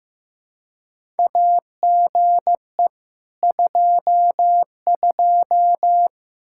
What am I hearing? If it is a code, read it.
Morse code, heard as a keyed tone, about 15 wpm, 700 Hz. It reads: AGE 22
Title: Practice QSO - Exchange 3 - More Info - 15wpm